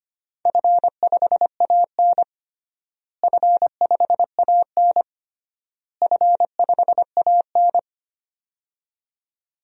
Morse code, heard as a keyed tone, about 25 wpm, 700 Hz. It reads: F5AN F5AN F5AN
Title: DXCC-100 Call Signs - 4 Characters - Repeated 3x - 25wpm